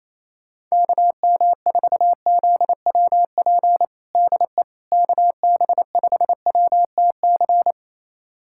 Morse code, heard as a keyed tone, about 27 wpm, 700 Hz. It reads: KM4ZWP DE KB5WTC
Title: USA Callsign DE Callsign 28wpm